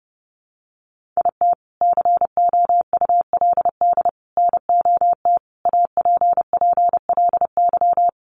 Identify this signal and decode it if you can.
Morse code, heard as a keyed tone, about 28 wpm, 700 Hz. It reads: IT COULD NOT APPLY
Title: Sentences from Top 1200 Words 30wpm